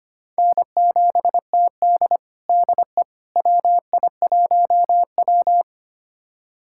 Morse code, heard as a keyed tone, about 25 wpm, 700 Hz. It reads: N7TD DE WI1W
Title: USA Callsign DE Callsign - 4 Characters - 25wpm